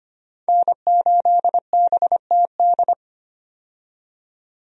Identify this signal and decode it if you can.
Morse code, heard as a keyed tone, about 25 wpm, 700 Hz. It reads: N8BTD
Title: USA Call Signs - 5 Characters - 25wpm